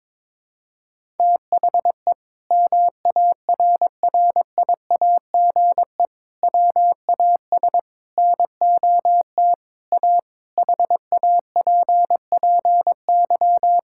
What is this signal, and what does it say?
Morse code, heard as a keyed tone, about 22 wpm, 700 Hz. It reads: THE MARRIAGE WAS NOT A HAPPY
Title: Sentences from Top 1000 Words - Review Part 5 - 22wpm